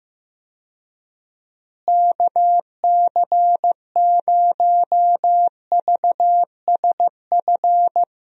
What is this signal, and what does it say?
Morse code, heard as a keyed tone, about 15 wpm, 700 Hz. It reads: KC0VSF
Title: USA Call Signs - 6 Characters - Repeated 2x - 15wpm